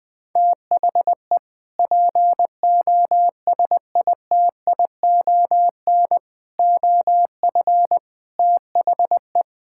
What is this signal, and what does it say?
Morse code, heard as a keyed tone, about 20 wpm, 700 Hz. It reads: THE POSITION OF THE
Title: Sentences from Top 1600 Words 20wpm